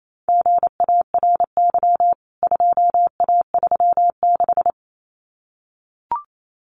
Morse code, heard as a keyed tone, about 28 wpm, 700 Hz. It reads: GARY 2A36
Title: CWT Exchange 28wpm